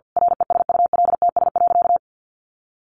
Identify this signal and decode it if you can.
Morse code, heard as a keyed tone, about 50 wpm, 700 Hz. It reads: RESULTS.
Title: A Sound Of Thunder 50wpm